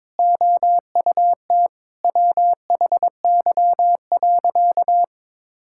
Morse code, heard as a keyed tone, about 22 wpm, 700 Hz. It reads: OUT WHY.
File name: practice / 22wpm / Sentences with 3-Word Phrases - 22wpm